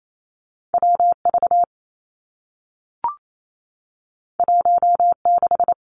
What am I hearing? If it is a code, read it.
Morse code, heard as a keyed tone, about 28 wpm, 700 Hz. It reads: W V 1 6
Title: Two Challenging Letter-Number Combinations 28wpm